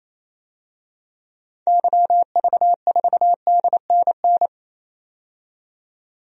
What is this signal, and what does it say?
Morse code, heard as a keyed tone, about 28 wpm, 700 Hz. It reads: YV4DNN
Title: DXCC-100 Call Signs - 6 Characters - 28wpm